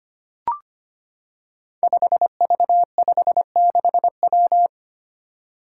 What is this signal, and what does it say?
Morse code, heard as a keyed tone, about 25 wpm, 700 Hz. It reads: 5V56W
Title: Five Challenging Letter-Number Combinations 25wpm